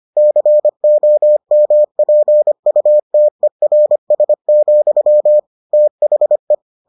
Morse code, heard as a keyed tone, about 25 wpm, 600 Hz. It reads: COMPUTERS, THE